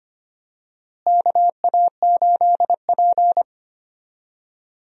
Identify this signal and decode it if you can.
Morse code, heard as a keyed tone, about 25 wpm, 700 Hz. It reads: KA8P
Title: USA Call Signs - 4 Characters - Repeated 2x - 25wpm